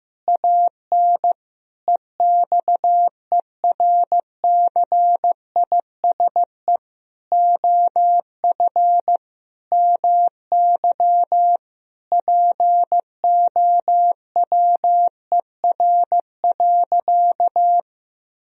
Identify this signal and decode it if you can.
Morse code, heard as a keyed tone, about 16 wpm, 700 Hz. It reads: AN EXERCISE OF MY POWER.
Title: Sentences from Top 1000 Words - Review Part 4 - 15wpm